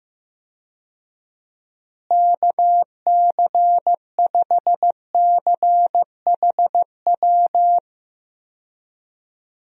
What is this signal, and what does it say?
Morse code, heard as a keyed tone, about 15 wpm, 700 Hz. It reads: KC5CHW
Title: USA Call Signs - 6 Characters - Repeated 3x - 15wpm